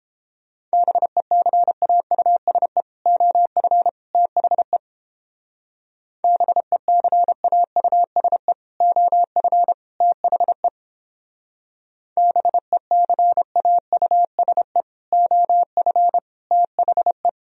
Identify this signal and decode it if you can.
Morse code, heard as a keyed tone, about 29 wpm, 700 Hz. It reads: BECAUSE OF THE BECAUSE OF THE BECAUSE OF THE